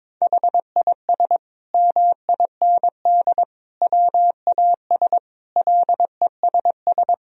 Morse code, heard as a keyed tone, about 22 wpm, 700 Hz. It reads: HIS MIND WAS LESS
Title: Sentences from Top 1000 Words - Review Part 2 - 22wpm